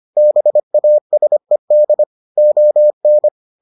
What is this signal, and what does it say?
Morse code, heard as a keyed tone, about 25 wpm, 600 Hz. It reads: BASED ON